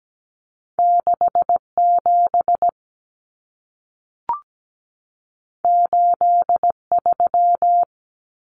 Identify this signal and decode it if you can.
Morse code, heard as a keyed tone, about 17 wpm, 700 Hz. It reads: 67 83
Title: Numbers - 2-Digit 17wpm